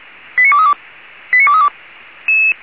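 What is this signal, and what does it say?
Five-tone selective call (ZVEI), heard as quick runs of five tones, tones of about 70 ms. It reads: 89232 89232
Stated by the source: ZVEI five-tone selective call received by SDR.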